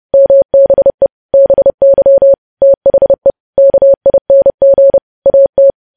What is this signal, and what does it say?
Morse code, heard as a keyed tone, about 30 wpm, 550 Hz. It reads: MBE BY THE KING AT